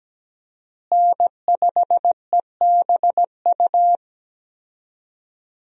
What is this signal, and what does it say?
Morse code, heard as a keyed tone, about 17 wpm, 700 Hz. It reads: N5EBU
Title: USA Call Signs - 5 Characters - Repeated 3x - 17wpm